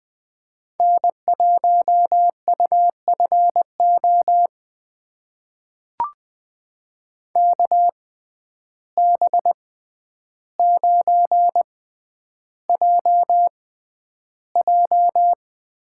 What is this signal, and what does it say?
Morse code, heard as a keyed tone, about 20 wpm, spaced as for 10 wpm, 700 Hz. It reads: N1UFO KB9JJ
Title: USA Call Signs - 6x Character Spacing - 20wpm